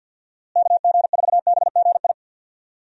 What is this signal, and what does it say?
Morse code, heard as a keyed tone, about 48 wpm, 700 Hz. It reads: KG4BGI
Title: USA Call Signs - 6 Characters - Repeated 2x - 50wpm